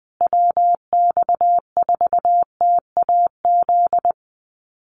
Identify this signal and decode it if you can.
Morse code, heard as a keyed tone, about 20 wpm, 700 Hz. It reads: WX4TAZ
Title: USA Call Signs Repeated 2x 20wpm